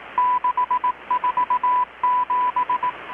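Morse code, heard as a keyed tone, about 18 wpm, 990 Hz, with radio noise. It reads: 647
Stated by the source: Morse from KPH's Night of Nights XX maritime radiotelegraphy event, received in Chiapas, Mexico about 3,800 km away on a Kenwood R-600 with a grounded 10 m long-wire antenna.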